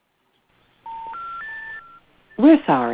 Special information tone, with noise.